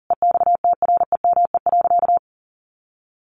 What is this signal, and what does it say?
Morse code, heard as a keyed tone, about 40 wpm, 700 Hz. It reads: EXTREME.
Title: Sentences from Top 1500 Words 40wpm